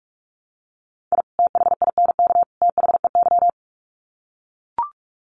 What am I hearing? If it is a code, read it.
Morse code, heard as a keyed tone, about 44 wpm, 700 Hz. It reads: I THINK THEY
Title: Sets of 3 Words 45wpm